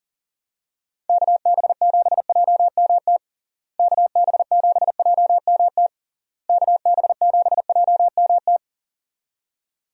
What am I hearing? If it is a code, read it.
Morse code, heard as a keyed tone, about 40 wpm, 700 Hz. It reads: KB7JMT KB7JMT KB7JMT
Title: USA Call Signs - 6 Characters - Repeated 3x - 40wpm